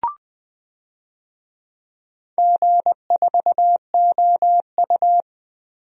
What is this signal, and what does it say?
Morse code, heard as a keyed tone, about 20 wpm, 700 Hz. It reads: G4OU
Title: DXCC-100 Call Signs - 4 Characters - 20wpm